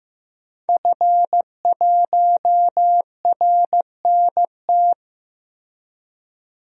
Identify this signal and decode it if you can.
Morse code, heard as a keyed tone, about 15 wpm, 700 Hz. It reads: F1RNT
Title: DXCC-100 Call Signs - 5 Characters - 15wpm